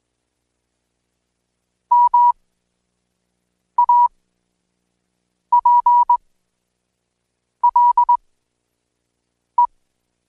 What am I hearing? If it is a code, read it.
Morse code, heard as a keyed tone, about 19 wpm, spaced as for 5 wpm, 970 Hz. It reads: MAPLE
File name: Code Quick